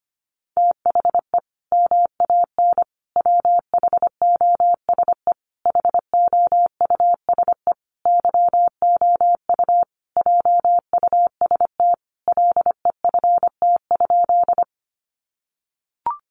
Morse code, heard as a keyed tone, about 25 wpm, 700 Hz. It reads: THE MAN WHOSE HOUSE YOU JUST LEFT?
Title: Sentences from Top 1200 Words 25wpm